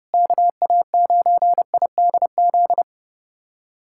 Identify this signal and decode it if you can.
Morse code, heard as a keyed tone, about 32 wpm, 700 Hz. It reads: KA9IDZ
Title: USA Callsign DE Callsign - 6 Characters - 30wpm